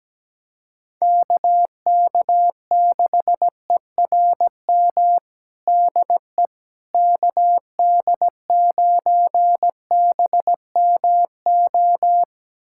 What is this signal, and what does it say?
Morse code, heard as a keyed tone, about 17 wpm, 700 Hz. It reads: KK6ERM DE KD9BMO
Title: USA Callsign DE Callsign - 6 Characters - 17wpm